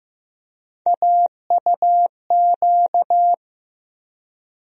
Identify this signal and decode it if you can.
Morse code, heard as a keyed tone, about 15 wpm, 700 Hz. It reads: AUQ
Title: USA Call Sign Postfixes - 3 Letters 15wpm